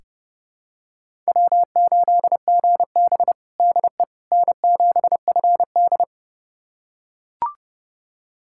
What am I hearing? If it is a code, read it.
Morse code, heard as a keyed tone, about 32 wpm, 700 Hz. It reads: W8GB DE N7FD
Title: USA Callsign DE Callsign - 4 Characters - 30wpm